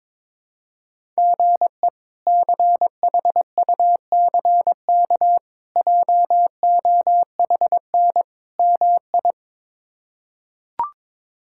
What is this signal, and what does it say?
Morse code, heard as a keyed tone, about 21 wpm, 700 Hz. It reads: GE CHUCK JOHN MI
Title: K1USN SST QSO Practice - 22wpm